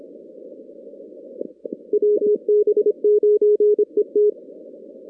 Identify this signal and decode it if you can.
Morse code, heard as a keyed tone, about 27 wpm, 420 Hz. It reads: WB9ET
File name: 2016 arrl dx cw / 2016-ARRL-DX-CW-K5TR- - 0-46